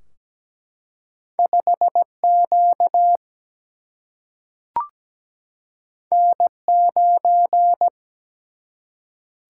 Morse code, heard as a keyed tone, about 17 wpm, 700 Hz. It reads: 5Q N9
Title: Single-Letter Single-Number 17wpm